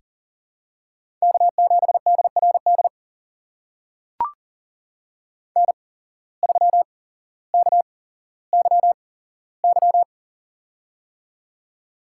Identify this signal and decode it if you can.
Morse code, heard as a keyed tone, about 40 wpm, spaced as for 17 wpm, 700 Hz. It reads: K7DRD N3KYY
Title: USA Call Signs - 5 Characters - 8x Character Spacing - 40wpm